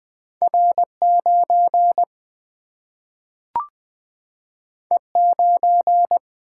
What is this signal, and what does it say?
Morse code, heard as a keyed tone, about 20 wpm, 700 Hz. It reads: R 9 E 9
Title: Single-Letter Single-Number 20wpm